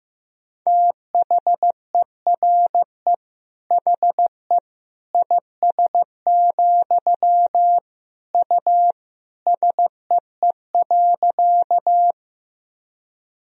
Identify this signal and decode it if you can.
Morse code, heard as a keyed tone, about 15 wpm, 700 Hz. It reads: THERE HE IS, U SEE.
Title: Sentences from Top 100 Words and Abbreviations 15wpm